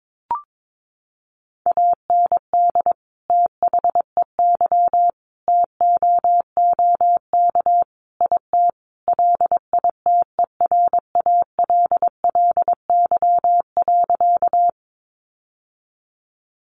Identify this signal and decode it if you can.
Morse code, heard as a keyed tone, about 22 wpm, 700 Hz. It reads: AND THEY TOOK IT LITERALLY.
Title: Sentences from Top 1400 Words 22wpm